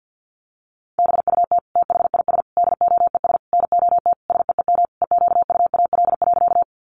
Morse code, heard as a keyed tone, about 50 wpm, 700 Hz. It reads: BUT THIS DOES NOT SEEM EQUAL.
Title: Sentences from Top 1400 Words 50wpm